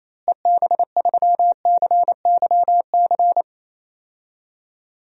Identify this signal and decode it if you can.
Morse code, heard as a keyed tone, about 28 wpm, 700 Hz. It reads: EB3CYC
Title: DXCC-100 Call Signs - 6 Characters - Repeated 3x - 28wpm